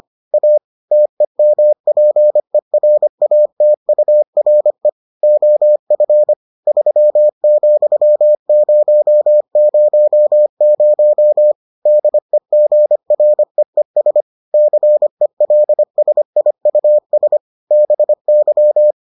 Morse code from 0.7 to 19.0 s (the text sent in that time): TEMPERATURE OF 3,000 DEGREES CELSIUS BY